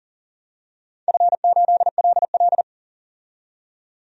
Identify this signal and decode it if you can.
Morse code, heard as a keyed tone, about 42 wpm, 700 Hz. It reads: F8LL
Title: DXCC-100 Call Signs - 4 Characters - 40wpm